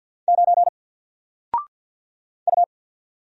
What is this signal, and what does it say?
Morse code, heard as a keyed tone, about 48 wpm, 700 Hz. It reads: 9 U
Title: Single Letter-Number Alt 50wpm